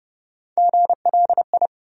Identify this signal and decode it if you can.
Morse code, heard as a keyed tone, about 30 wpm, 700 Hz. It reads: GLI
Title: USA Call Sign Postfixes - 3 Letters 30wpm